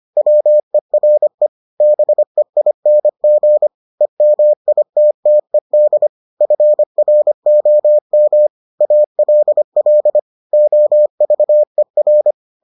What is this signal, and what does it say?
Morse code, heard as a keyed tone, about 25 wpm, 600 Hz. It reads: WERE BEING EMITTED FROM ALL OVER